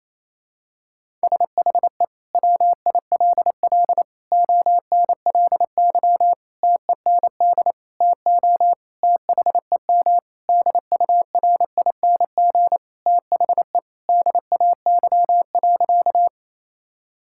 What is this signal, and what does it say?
Morse code, heard as a keyed tone, about 28 wpm, 700 Hz. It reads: SHE WILL ONLY TEND TO THEM DURING THE DAY.
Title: Sentences from Top 1000 Words - Review Part 5 - 28wpm